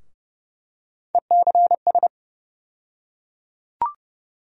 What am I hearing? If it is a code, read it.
Morse code, heard as a keyed tone, about 30 wpm, 700 Hz. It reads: ECS
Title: USA Call Sign Postfixes - 3 Letters 30wpm